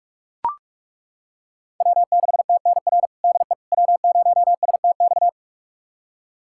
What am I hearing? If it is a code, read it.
Morse code, heard as a keyed tone, about 43 wpm, 700 Hz. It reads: W6TNR DE W0STX
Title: USA Callsign DE Callsign - 5 Characters - 45wpm